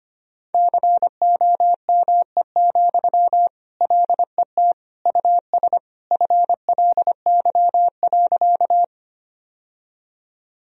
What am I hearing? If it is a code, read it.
Morse code, heard as a keyed tone, about 25 wpm, 700 Hz. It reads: COME, LET US FLY.
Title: Sentences from Top 1000 Words - Review Part 2 - 25wpm